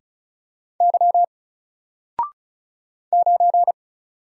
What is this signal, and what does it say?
Morse code, heard as a keyed tone, about 36 wpm, 700 Hz. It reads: Y 9
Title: Single Letter-Number Alt 35wpm